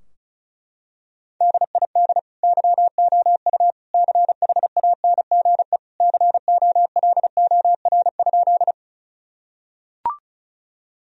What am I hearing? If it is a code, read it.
Morse code, heard as a keyed tone, about 36 wpm, 700 Hz. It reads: DID YOU CHANGE COLOR?
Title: Sentences from Top 1000 Words - Review Part 2 - 35wpm